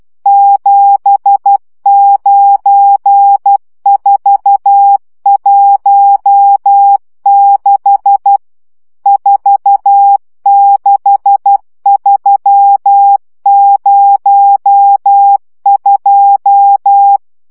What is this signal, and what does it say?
Morse code, heard as a keyed tone, about 12 wpm, 800 Hz. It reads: 79416 46302
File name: morze vizsga 2011-2 10